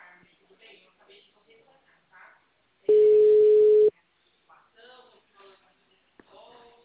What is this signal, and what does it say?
Ringback tone (Europe, 425 Hz), with speech.
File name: out- -2002- - - .66447